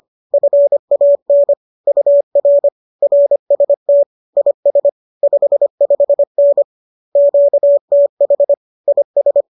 Morse code, heard as a keyed tone, about 25 wpm, 570 Hz. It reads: FAN UR RST IS 55N QTH IS
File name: qso203 25